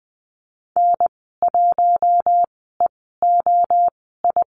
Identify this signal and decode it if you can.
Morse code, heard as a keyed tone, about 20 wpm, spaced as for 14 wpm, 700 Hz. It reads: N1EOI
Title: 042 Lesson 8g - Number 1 - Call Signs 20wpm